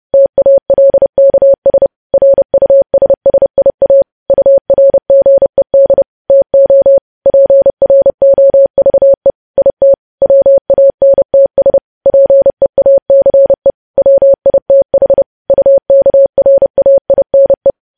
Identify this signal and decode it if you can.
Morse code, heard as a keyed tone, about 30 wpm, 550 Hz. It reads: TALKS RUSSIA URGED TO PROVE IT WANTS PEACE WITH UKRAINE
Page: news headlines in morse code at 30 wpm